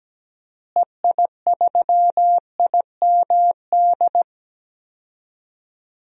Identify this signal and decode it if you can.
Morse code, heard as a keyed tone, about 17 wpm, 700 Hz. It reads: EI3IMD
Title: DXCC-100 Call Signs - 6 Characters - Repeated 2x - 17wpm